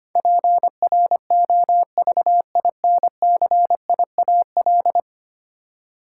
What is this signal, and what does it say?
Morse code, heard as a keyed tone, about 25 wpm, 700 Hz. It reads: PROVINCIAL
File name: 10 Letter Words - First 3 Letters Given - 25wpm